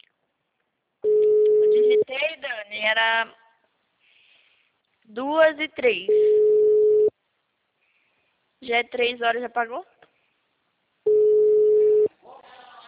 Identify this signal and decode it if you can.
Ringback tone (Europe, 425 Hz), with speech.